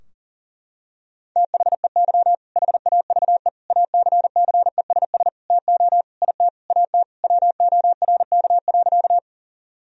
Morse code, heard as a keyed tone, about 37 wpm, 700 Hz. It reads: THEY HAVE ACCESS TO IT AT WORK.
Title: Sentences from Top 1000 Words - Review Part 4 - 40wpm